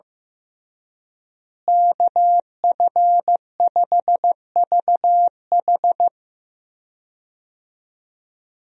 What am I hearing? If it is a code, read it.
Morse code, heard as a keyed tone, about 15 wpm, 700 Hz. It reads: KF5VH
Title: USA Call Signs - 5 Characters - Repeated 3x - 15wpm